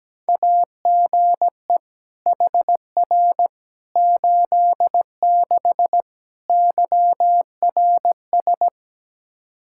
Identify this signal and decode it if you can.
Morse code, heard as a keyed tone, about 17 wpm, 700 Hz. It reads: AGE HR 86 YRS
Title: Practice QSO - Exchange 3 - More Info - 17wpm